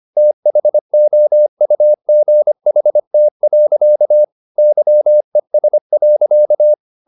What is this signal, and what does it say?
Morse code, heard as a keyed tone, about 25 wpm, 600 Hz. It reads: THOUGHT. YES.